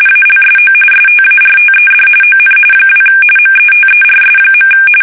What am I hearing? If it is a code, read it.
RTTY, about 45.45 baud. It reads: ALSO STARTED ON 2 METERS.
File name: fsk850 / teletype-RTTY